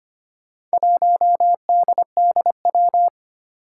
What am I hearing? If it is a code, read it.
Morse code, heard as a keyed tone, about 25 wpm, 700 Hz. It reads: 1DDW
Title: Four Challenging Letter-Number Combinations 25wpm